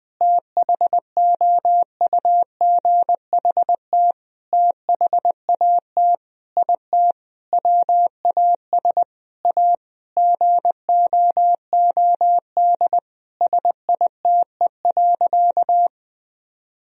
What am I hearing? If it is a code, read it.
Morse code, heard as a keyed tone, about 20 wpm, 700 Hz. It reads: THOUGHT THAT IT WAS A GOOD SITE.
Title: Sentences from Top 1000 Words - Review Part 4 - 20wpm